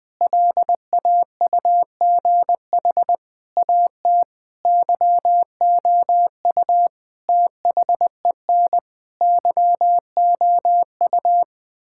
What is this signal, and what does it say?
Morse code, heard as a keyed tone, about 20 wpm, 700 Hz. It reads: LAUGH AT YOU THEN YOU
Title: Sentences with 8-Word Phrases - Spoken Beginning - 20wpm